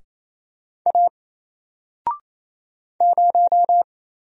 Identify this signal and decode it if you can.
Morse code, heard as a keyed tone, about 28 wpm, 700 Hz. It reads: A 0
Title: Single Character (Letter:Number:Punctuation:Prosign) 28wpm